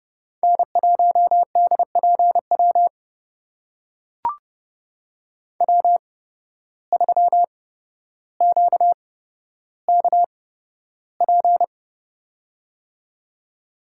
Morse code, heard as a keyed tone, about 30 wpm, spaced as for 13 wpm, 700 Hz. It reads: N1DPW W3QKP
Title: USA Call Signs - 5 Characters - 8x Character Spacing - 30wpm